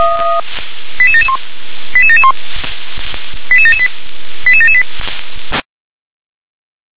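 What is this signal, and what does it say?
Five-tone selective call (ZVEI), heard as quick runs of five tones, tones of about 70 ms. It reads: 88771 88771 88778 88778
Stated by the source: ZVEI five-tone selective call received by SDR.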